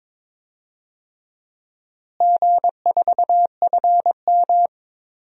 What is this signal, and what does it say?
Morse code, heard as a keyed tone, about 22 wpm, 700 Hz. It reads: G4FM
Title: DXCC-100 Call Signs - 4 Characters - Repeated 2x - 22wpm